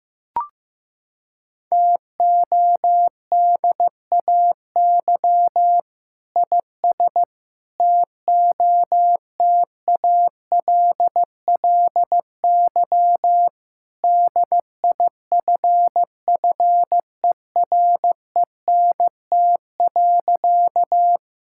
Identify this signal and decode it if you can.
Morse code, heard as a keyed tone, about 16 wpm, 700 Hz. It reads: TODAY IS TOTALLY DIFFERENT.
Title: Sentences from Top 1600 Words 15wpm